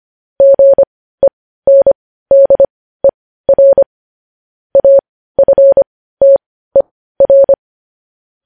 Morse code, heard as a keyed tone, about 25 wpm, spaced as for 15 wpm, 550 Hz. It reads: GENDER AFTER
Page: news headlines in morse code at 15 wpm